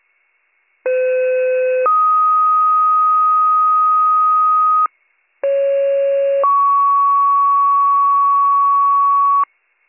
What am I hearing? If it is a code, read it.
Two-tone page, heard as a one-second tone, then a three-second tone, about 510 then 1250 Hz, about 550 then 1120 Hz. It is heard from 0.9 to 9.4 s.